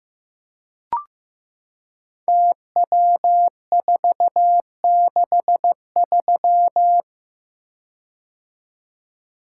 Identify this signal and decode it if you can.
Morse code, heard as a keyed tone, about 15 wpm, 700 Hz. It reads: TW463
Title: Code Groups - 5 Letter:Numbers - 15wpm